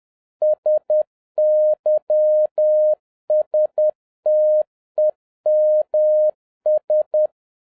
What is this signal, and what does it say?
Morse code, heard as a keyed tone, about 10 wpm, 610 Hz. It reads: SYSTEMS
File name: 2011-07-mp3 / n46 JOHN K3TN 10wpm